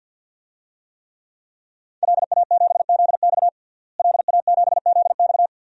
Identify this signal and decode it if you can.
Morse code, heard as a keyed tone, about 50 wpm, 700 Hz. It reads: LA7ZX LA7ZX
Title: DXCC-100 Call Signs - 5 Characters - Repeated 2x - 50wpm